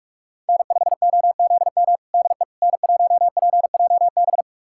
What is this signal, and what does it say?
Morse code, heard as a keyed tone, about 41 wpm, 700 Hz. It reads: N5OZM DE N1PJB
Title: USA Callsign DE Callsign - 5 Characters - 45wpm